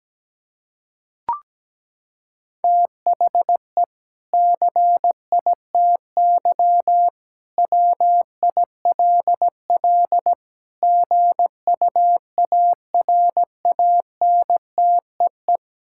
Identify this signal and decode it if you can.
Morse code, heard as a keyed tone, about 17 wpm, 700 Hz. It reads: THE CITY WILL GUARANTEE
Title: Sentences from Top 1600 Words 17wpm